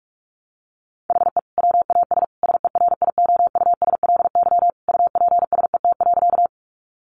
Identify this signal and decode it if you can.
Morse code, heard as a keyed tone, about 44 wpm, 700 Hz. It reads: HE WAS SERIOUSLY UPSET.
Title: Sentences from Top 1200 Words 45wpm